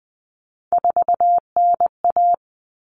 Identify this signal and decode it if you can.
Morse code, heard as a keyed tone, about 20 wpm, 700 Hz. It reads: 4NA
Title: Common Cut Numbers - 3-Digit 20wpm